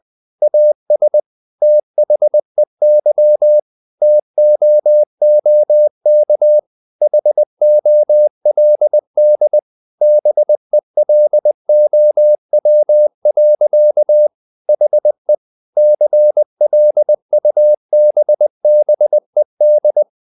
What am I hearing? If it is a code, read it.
Morse code, heard as a keyed tone, about 20 wpm, 600 Hz. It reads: AS THEY TOOK HOLD BELOW. HE CLUBBED
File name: the old man and the sea part 2 20wpm